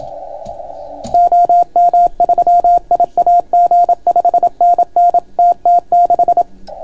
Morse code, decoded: OM3IAG 5NN TT6